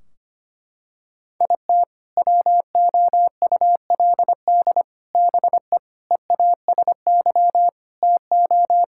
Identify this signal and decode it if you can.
Morse code, heard as a keyed tone, about 25 wpm, 700 Hz. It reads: IT WOULD BE EASY TO
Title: Sentences from Top 1600 Words 25wpm